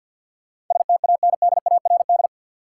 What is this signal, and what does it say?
Morse code, heard as a keyed tone, about 53 wpm, 700 Hz. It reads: STANDARD
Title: Sentences from Top 1000 Words - Review Part 6 - New Word Drill - 50wpm